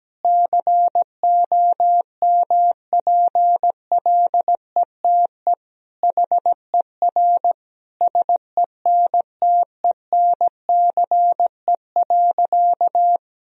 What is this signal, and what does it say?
Morse code, heard as a keyed tone, about 17 wpm, 700 Hz. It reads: COMPLETE HER SENTENCE.
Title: Sentences from Top 1000 Words - Review Part 2 - 17wpm